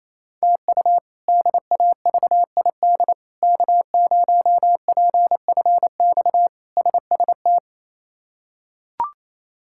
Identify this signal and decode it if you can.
Morse code, heard as a keyed tone, about 28 wpm, 700 Hz. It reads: TU DAVID K0PFX SST
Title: K1USN SST QSO Practice - 28wpm